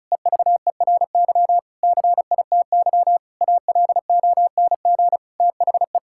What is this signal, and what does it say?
Morse code, heard as a keyed tone, about 36 wpm, 700 Hz. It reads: EVERY CITY ALONG THE
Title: Sentences from Top 1000 Words - Review Part 6 - 35wpm